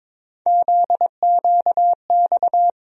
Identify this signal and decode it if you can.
Morse code, heard as a keyed tone, about 22 wpm, 700 Hz. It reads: ZQX
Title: USA Call Sign Postfixes - 3 Letters 22wpm